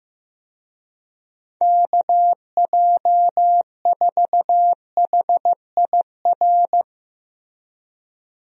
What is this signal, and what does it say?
Morse code, heard as a keyed tone, about 16 wpm, 700 Hz. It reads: KJ4HIR
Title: USA Call Signs - 6 Characters - Repeated 3x - 15wpm